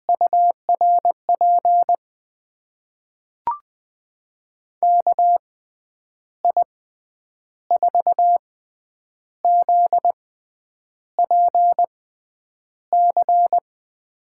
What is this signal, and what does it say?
Morse code, heard as a keyed tone, about 20 wpm, spaced as for 8 wpm, 700 Hz. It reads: URP KI4ZPC
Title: USA Call Signs - 6 Characters - 6x Character Spacing - 20wpm